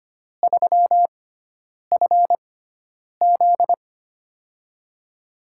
Morse code, heard as a keyed tone, about 25 wpm, spaced as for 9 wpm, 700 Hz. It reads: 3FZ
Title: USA Call Signs - 4 Characters - 6x Character Spacing - 25wpm